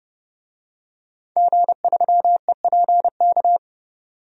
Morse code, heard as a keyed tone, about 30 wpm, 700 Hz. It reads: G3EPK
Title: DXCC-100 Call Signs - 5 Characters - 30wpm